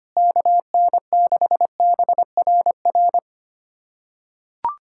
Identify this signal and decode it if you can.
Morse code, heard as a keyed tone, about 25 wpm, 700 Hz. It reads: KN6BRR
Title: USA Callsign DE Callsign 25wpm